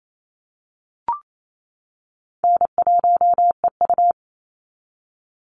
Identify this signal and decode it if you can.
Morse code, heard as a keyed tone, about 28 wpm, 700 Hz. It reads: N1EU
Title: NA SOTA Activator Callsigns 28wpm